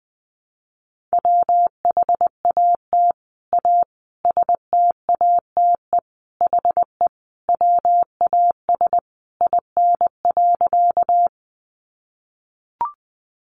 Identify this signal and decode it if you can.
Morse code, heard as a keyed tone, about 20 wpm, 700 Hz. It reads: WHAT A STATE HE WAS IN.
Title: Sentences from Top 400 Words 20wpm